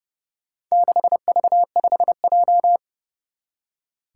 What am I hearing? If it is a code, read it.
Morse code, heard as a keyed tone, about 30 wpm, 700 Hz. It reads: 6V5J
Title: Four Challenging Letter-Number Combinations 30wpm